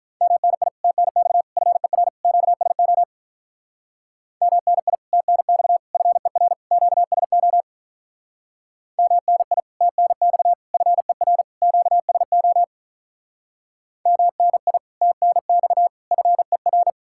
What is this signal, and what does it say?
Morse code, heard as a keyed tone, about 45 wpm, 700 Hz. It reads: MNI TNX FER QSO MNI TNX FER QSO MNI TNX FER QSO MNI TNX FER